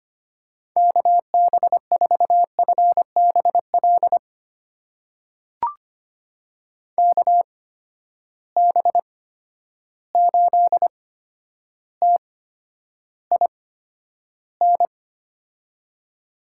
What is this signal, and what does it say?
Morse code, heard as a keyed tone, about 25 wpm, spaced as for 11 wpm, 700 Hz. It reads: KB4FBL KB8TIN
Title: USA Call Signs - 6 Characters - 8x Character Spacing - 25wpm